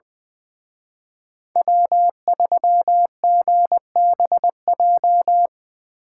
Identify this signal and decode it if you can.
Morse code, heard as a keyed tone, about 22 wpm, 700 Hz. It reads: W3GBJ
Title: Speed Racing - USA 1x3 Call Signs 20wpm